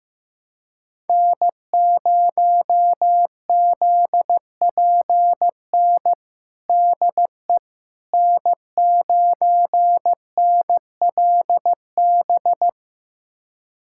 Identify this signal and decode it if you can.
Morse code, heard as a keyed tone, about 15 wpm, 700 Hz. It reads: N0ZPN DE N9NLB
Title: USA Callsign DE Callsign 15wpm